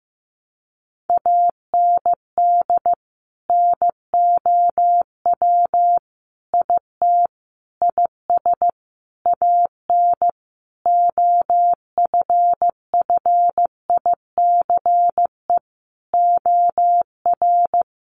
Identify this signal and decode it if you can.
Morse code, heard as a keyed tone, about 15 wpm, 700 Hz. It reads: AND NOW IT IS AN OFFICE OR
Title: Sentences from Top 1000 Words 15wpm